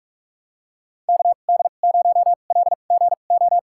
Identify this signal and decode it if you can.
Morse code, heard as a keyed tone, about 46 wpm, spaced as for 32 wpm, 700 Hz. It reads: KD0LGO
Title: USA Call Signs - 6 Characters - 2x Character Spacing - 45wpm